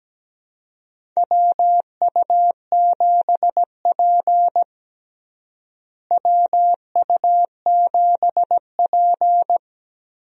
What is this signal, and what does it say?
Morse code, heard as a keyed tone, about 17 wpm, 700 Hz. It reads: WU7P WU7P
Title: USA Call Signs - 4 Characters - Repeated 2x - 17wpm